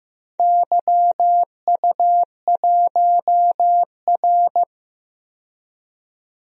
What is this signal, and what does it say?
Morse code, heard as a keyed tone, about 16 wpm, 700 Hz. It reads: YU1R
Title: DXCC-100 Call Signs - 4 Characters - 15wpm